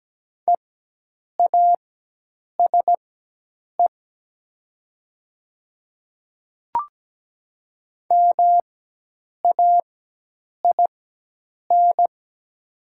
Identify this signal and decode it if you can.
Morse code, heard as a keyed tone, about 17 wpm, spaced as for 8 wpm, 700 Hz. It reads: EASE MAIN
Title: 4x Character Spacing - Top 500 Words - 17wpm